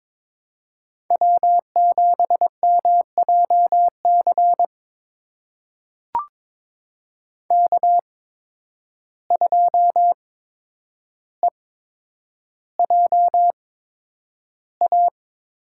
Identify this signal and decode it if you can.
Morse code, heard as a keyed tone, about 22 wpm, spaced as for 9 wpm, 700 Hz. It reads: W7MJC K2EJA
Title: USA Call Signs - 5 Characters - 8x Character Spacing - 22wpm